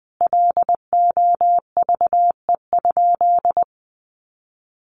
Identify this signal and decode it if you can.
Morse code, heard as a keyed tone, about 20 wpm, 700 Hz. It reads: LOVE?
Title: Sentences from Top 1300 Words 20wpm